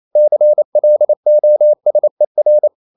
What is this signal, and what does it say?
Morse code, heard as a keyed tone, about 28 wpm, 600 Hz. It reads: CLOSER
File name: CW / Practice / EnglishQuotations0128